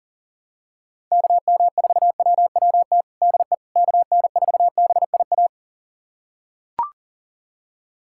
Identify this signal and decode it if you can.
Morse code, heard as a keyed tone, about 37 wpm, 700 Hz. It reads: KM4WWT DE KN4BIA
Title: USA Callsign DE Callsign 40wpm